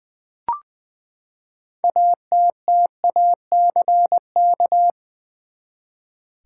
Morse code, heard as a keyed tone, about 20 wpm, 700 Hz. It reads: ATTACK
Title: Six Letter Words - 20wpm